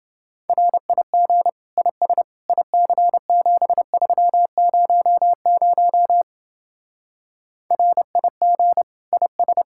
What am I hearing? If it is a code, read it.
Morse code, heard as a keyed tone, about 29 wpm, 700 Hz. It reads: RIG IS IC7300 RIG IS